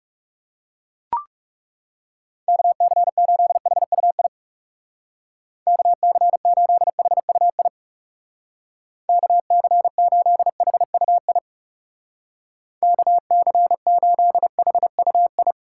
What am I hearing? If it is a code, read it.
Morse code, heard as a keyed tone, about 37 wpm, 700 Hz. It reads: KC8HUI KC8HUI KC8HUI KC8HUI
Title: Speed Racing - USA 2x3 Call Signs 30wpm